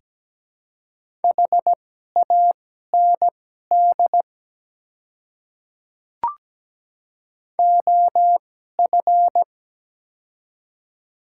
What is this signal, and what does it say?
Morse code, heard as a keyed tone, about 17 wpm, spaced as for 12 wpm, 700 Hz. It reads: HAND OF
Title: 2x Character Spacing - Top 300 Words - 17wpm